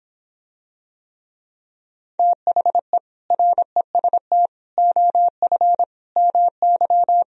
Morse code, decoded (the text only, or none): THE REST OF MY